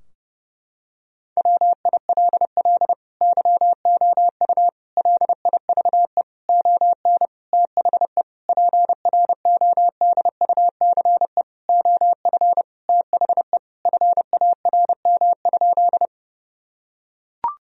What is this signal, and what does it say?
Morse code, heard as a keyed tone, about 30 wpm, 700 Hz. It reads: WILL YOU LIVE ON THE PRODUCE OF THE FARM?
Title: Sentences from Top 1000 Words - Review Part 3 - 30wpm